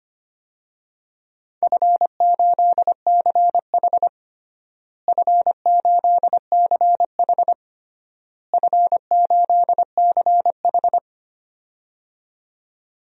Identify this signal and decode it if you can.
Morse code, heard as a keyed tone, about 25 wpm, 700 Hz. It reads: F8CH F8CH F8CH
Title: DXCC-100 Call Signs - 4 Characters - Repeated 3x - 25wpm